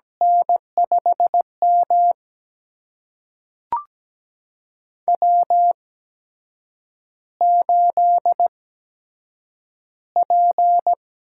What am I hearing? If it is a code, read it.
Morse code, heard as a keyed tone, about 17 wpm, spaced as for 7 wpm, 700 Hz. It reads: N5M W8P
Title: USA Call Signs - 4 Characters - 8x Character Spacing - 17wpm